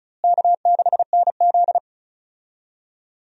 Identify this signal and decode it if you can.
Morse code, heard as a keyed tone, about 34 wpm, 700 Hz. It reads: K6NZ
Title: USA Call Signs - 4 Characters - Repeated 3x - 35wpm